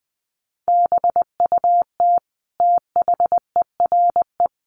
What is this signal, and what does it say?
Morse code, heard as a keyed tone, about 20 wpm, 700 Hz. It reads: BUT THERE
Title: Sentences from Top 100 Words 20wpm